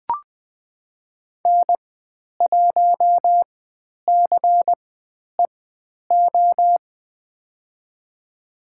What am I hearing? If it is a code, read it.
Morse code, heard as a keyed tone, about 20 wpm, spaced as for 10 wpm, 700 Hz. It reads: N1CEO
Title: 102 Lesson 17g - Letter u - Call Signs 20wpm